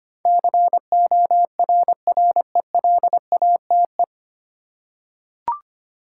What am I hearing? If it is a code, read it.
Morse code, heard as a keyed tone, about 25 wpm, 700 Hz. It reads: CORRELATE
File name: practice / 25wpm / 9 Letter Words - First 3 Letters Given - 25wpm